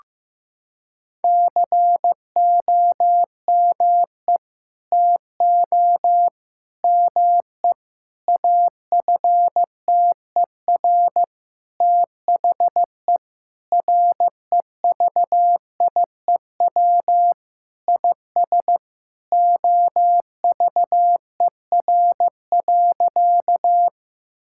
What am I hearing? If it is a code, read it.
Morse code, heard as a keyed tone, about 15 wpm, 700 Hz. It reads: COME TO ME AFTER THE REVIEW IS OVER.
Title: Sentences from Top 1000 Words - Review Part 6 - 15wpm